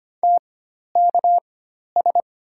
Morse code, heard as a keyed tone, about 25 wpm, spaced as for 12 wpm, 700 Hz. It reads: TKS
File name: Top 90 QSO Elements - 4x Character Spacing - 25wpm